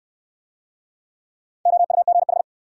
Morse code, heard as a keyed tone, about 58 wpm, 700 Hz. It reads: OUGH